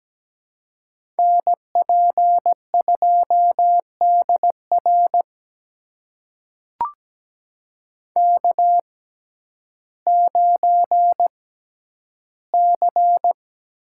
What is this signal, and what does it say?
Morse code, heard as a keyed tone, about 17 wpm, spaced as for 10 wpm, 700 Hz. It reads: NP2DR K9C
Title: USA Call Signs - 5 Characters - 6x Character Spacing - 17wpm